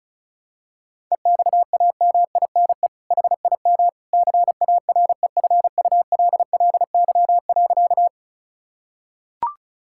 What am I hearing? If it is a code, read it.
Morse code, heard as a keyed tone, about 34 wpm, 700 Hz. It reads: EXAMINE HIM CAREFULLY.
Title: Sentences from Top 1600 Words 35wpm